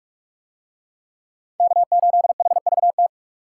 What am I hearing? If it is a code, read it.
Morse code, heard as a keyed tone, about 44 wpm, 700 Hz. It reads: K8HVT